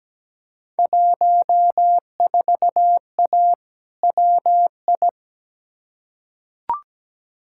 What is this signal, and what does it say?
Morse code, heard as a keyed tone, about 17 wpm, 700 Hz. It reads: 14A WI
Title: Field Day Exchanges Encore - 17wpm